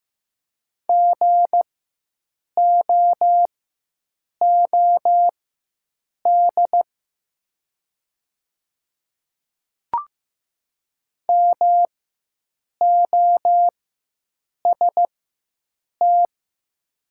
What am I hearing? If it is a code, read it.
Morse code, heard as a keyed tone, about 15 wpm, spaced as for 7 wpm, 700 Hz. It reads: GOOD MOST
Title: 4x Character Spacing - Top 200 Words - 15wpm